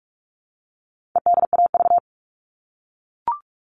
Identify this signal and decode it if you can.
Morse code, heard as a keyed tone, about 44 wpm, 700 Hz. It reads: EDAV
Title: All Cut Numbers - 4-Digit 45wpm